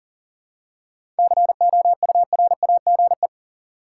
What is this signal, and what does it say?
Morse code, heard as a keyed tone, about 40 wpm, 700 Hz. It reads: COURAGE
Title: Seven Letter Words - 40wpm